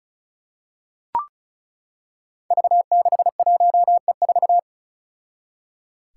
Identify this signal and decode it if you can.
Morse code, heard as a keyed tone, about 34 wpm, 700 Hz. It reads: V61E4
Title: Five Challenging Letter-Number Combinations 35wpm